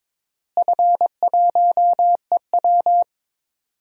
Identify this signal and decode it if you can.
Morse code, heard as a keyed tone, about 22 wpm, 700 Hz. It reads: F1EW
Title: DXCC-100 Call Signs - 4 Characters - Repeated 3x - 22wpm